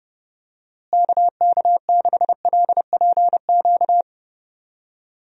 Morse code, heard as a keyed tone, about 30 wpm, 700 Hz. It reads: KK6LPQ
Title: USA Call Signs - 6 Characters - Repeated 2x - 30wpm